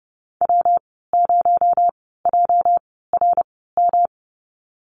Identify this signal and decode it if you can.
Morse code, heard as a keyed tone, about 30 wpm, spaced as for 17 wpm, 700 Hz. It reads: W0JRM
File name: Lesson 37g - Letter x - Call Signs 30wpm